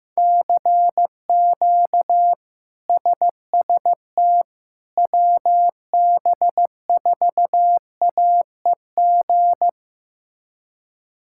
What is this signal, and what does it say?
Morse code, heard as a keyed tone, about 15 wpm, 700 Hz. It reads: CQ SST WB4AEG
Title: K1USN SST QSO Practice - 15wpm